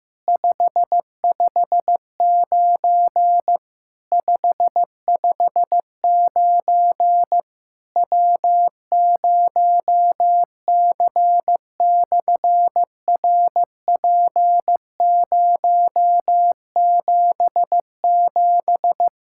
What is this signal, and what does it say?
Morse code, heard as a keyed tone, about 15 wpm, 700 Hz. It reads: 559 559 W0C/RP077
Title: SOTA QSO Practice 15wpm